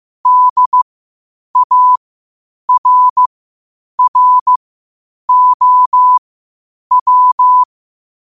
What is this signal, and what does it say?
Morse code, heard as a keyed tone, about 15 wpm, spaced as for 8 wpm, 1000 Hz. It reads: DARROW
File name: qotdfiles / 8wpm